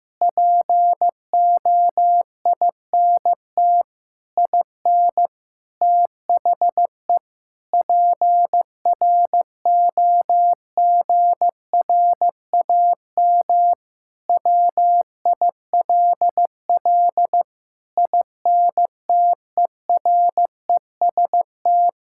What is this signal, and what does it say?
Morse code, heard as a keyed tone, about 15 wpm, 700 Hz. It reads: POINT IN THE PROGRAM WILL INTEREST
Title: Sentences from Top 1000 Words - Review Part 5 - 15wpm